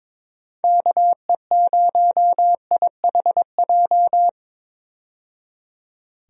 Morse code, heard as a keyed tone, about 22 wpm, 700 Hz. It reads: KE0IHJ
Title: USA Call Signs - 6 Characters - 22wpm